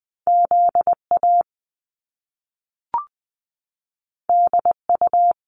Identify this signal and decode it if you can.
Morse code, heard as a keyed tone, about 20 wpm, 700 Hz. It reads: Z A D U
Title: Top Two Letter Combinations in Words 20wpm